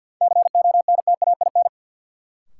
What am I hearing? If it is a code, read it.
Morse code, heard as a keyed tone, about 50 wpm, 700 Hz. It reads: CONTAIN
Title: Sentences from Top 500 Words - New Word Drill - 50wpm